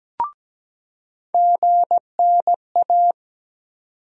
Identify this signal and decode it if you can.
Morse code, heard as a keyed tone, about 17 wpm, 700 Hz. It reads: GNA
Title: USA Call Sign Postfixes - 3 Letters 17wpm